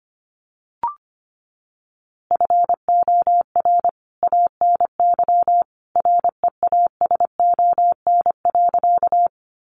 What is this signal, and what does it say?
Morse code, heard as a keyed tone, about 25 wpm, 700 Hz. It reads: FOR ANY REASON.
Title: A Sound Of Thunder 25wpm